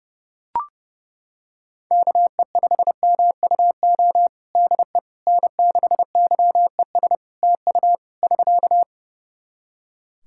Morse code, heard as a keyed tone, about 32 wpm, 700 Hz. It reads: KE5MUO DE N6YES TU <SK>
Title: K1USN SST QSO Practice 30wpm